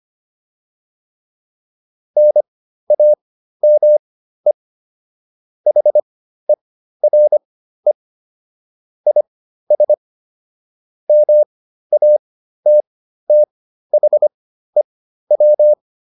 Morse code, heard as a keyed tone, about 25 wpm, spaced as for 13 wpm, 600 Hz. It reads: NAME HERE IS MATTHEW
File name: QSO303 13